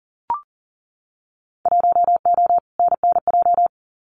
Morse code, heard as a keyed tone, about 39 wpm, 700 Hz. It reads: 1O NNJ
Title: Winter Field Day Exchanges 40wpm